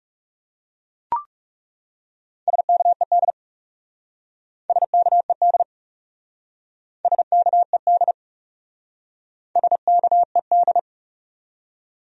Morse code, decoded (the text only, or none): SKED SKED SKED SKED